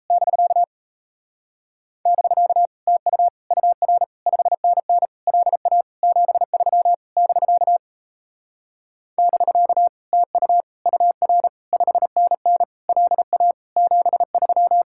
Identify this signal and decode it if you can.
Morse code, heard as a keyed tone, about 37 wpm, 700 Hz. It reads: <BK> <BK> TU UR 5NN LA 73 <BK> <BK> TU UR 5NN LA 73